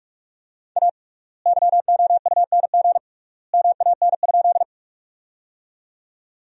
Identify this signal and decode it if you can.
Morse code, heard as a keyed tone, about 45 wpm, 700 Hz. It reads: A YOUNG MAN?
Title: 3x Word-Spacing - Sentences from Top 400 Words - 45wpm